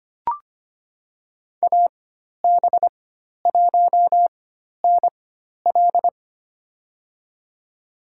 Morse code, heard as a keyed tone, about 25 wpm, spaced as for 12 wpm, 700 Hz. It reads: AB1NL
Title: USA Call Signs - 5 Characters - 4x Character Spacing - 25wpm